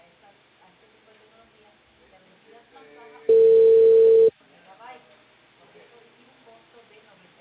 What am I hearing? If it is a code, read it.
Ringback tone (China, 450 Hz), with speech.